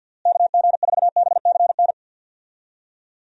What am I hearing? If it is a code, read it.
Morse code, heard as a keyed tone, about 52 wpm, 700 Hz. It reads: KG4BCN
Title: USA Call Signs - 6 Characters - Repeated 3x - 50wpm